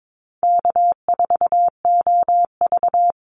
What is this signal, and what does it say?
Morse code, heard as a keyed tone, about 22 wpm, 700 Hz. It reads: K4OV
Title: Easy Call Signs Repeated 3x - 22wpm